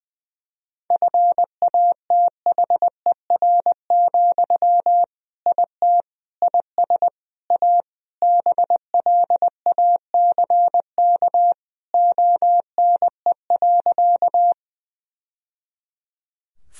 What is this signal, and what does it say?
Morse code, heard as a keyed tone, about 21 wpm, 700 Hz. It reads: FATHER, IT IS A BLACK ONE.
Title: Sentences from Top 1000 Words - Review Part 2 - 20wpm